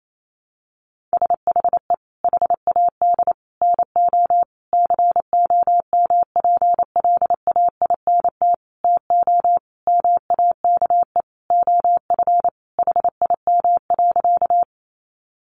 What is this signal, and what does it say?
Morse code, heard as a keyed tone, about 28 wpm, 700 Hz. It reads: SHE HAD NO COMPLAINT TO MAKE OF HIM.
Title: Sentences from Top 1400 Words 28wpm